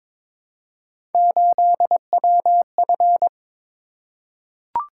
Morse code, heard as a keyed tone, about 22 wpm, 700 Hz. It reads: 8WF
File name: practice / 22wpm / Three Letter-Number Combinations - 22wpm